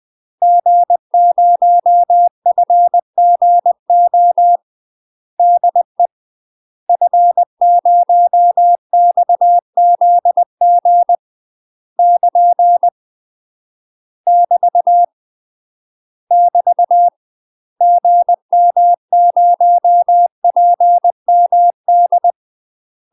Morse code, heard as a keyed tone, about 20 wpm, 700 Hz. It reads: G0FGO DE F0XZG <KN> <BT> <BT> GM0PMD
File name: UK MASTER QSOs TRAINING / 20WPM / UK MASTER QSOs 0059